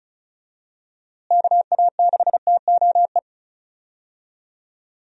Morse code, heard as a keyed tone, about 34 wpm, 700 Hz. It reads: KA6TOE
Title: USA Call Signs - 6 Characters - Repeated 2x - 35wpm